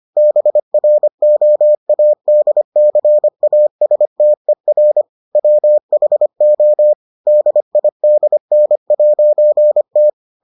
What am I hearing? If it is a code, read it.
Morse code, heard as a keyed tone, about 25 wpm, 600 Hz. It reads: BROADCASTER WHO DIDN'T